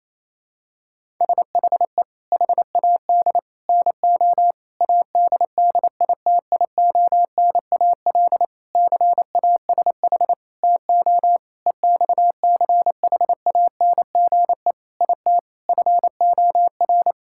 Morse code, heard as a keyed tone, about 27 wpm, 700 Hz. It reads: SHE HAD NO ADDITIONAL CASH TO EXCHANGE IT FOR
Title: Sentences from Top 1000 Words - Review Part 6 - 28wpm